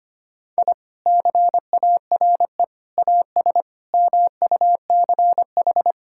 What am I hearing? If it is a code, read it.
Morse code, heard as a keyed tone, about 25 wpm, 700 Hz. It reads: I CARE AS MUCH
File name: practice / 25wpm / Sentences with 3-Word Phrases - 25wpm